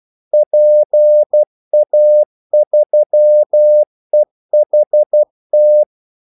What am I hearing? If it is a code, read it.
Morse code, heard as a keyed tone, about 12 wpm, 600 Hz. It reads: PA3EHT
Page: 2 ekstra ordmellemrum mellem kaldesignalerne